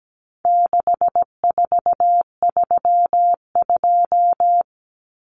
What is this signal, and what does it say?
Morse code, heard as a keyed tone, about 17 wpm, 700 Hz. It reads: 6432
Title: Common Cut Numbers - 4-Digit 17wpm